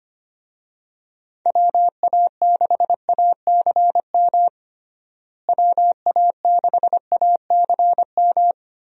Morse code, heard as a keyed tone, about 25 wpm, 700 Hz. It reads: WA6ACM WA6ACM
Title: USA Call Signs - 6 Characters - Repeated 3x - 25wpm